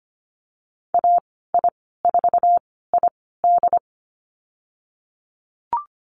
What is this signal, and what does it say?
Morse code, heard as a keyed tone, about 25 wpm, spaced as for 16 wpm, 700 Hz. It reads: AI4ID
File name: Lesson 15g - Number 5 - Call Signs 25wpm